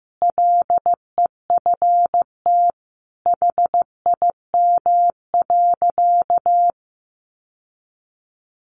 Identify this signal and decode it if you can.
Morse code, heard as a keyed tone, about 15 wpm, 700 Hz. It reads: LEFT HIM.
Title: Sentences from Top 1500 Words 15wpm